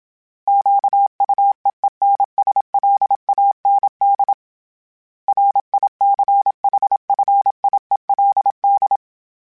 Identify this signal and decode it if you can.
Morse code, heard as a keyed tone, about 27 wpm, 800 Hz. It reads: QUEENSLAND RICHFIELD
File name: fromSMs / FromSMs mp3s / Cities@25